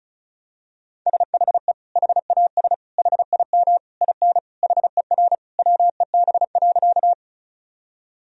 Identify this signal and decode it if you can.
Morse code, heard as a keyed tone, about 36 wpm, 700 Hz. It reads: SHE HAS HIM IN HER WEB.
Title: Sentences from Top 1000 Words - Review Part 4 - 35wpm